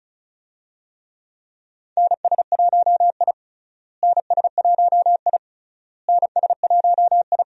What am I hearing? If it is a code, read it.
Morse code, heard as a keyed tone, about 36 wpm, 700 Hz. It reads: NS1I NS1I NS1I
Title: USA Call Signs - 4 Characters - Repeated 3x - 35wpm